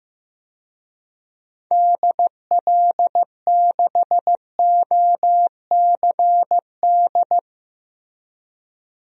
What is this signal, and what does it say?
Morse code, heard as a keyed tone, about 15 wpm, 700 Hz. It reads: DL6OCD
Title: DXCC-100 Call Signs - 6 Characters - Repeated 3x - 15wpm